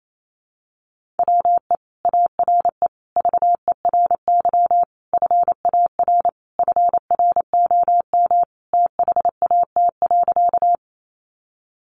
Morse code, heard as a keyed tone, about 28 wpm, 700 Hz. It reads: WE ARE VERY FAR FROM THAT.
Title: Sentences from Top 300 Words 28wpm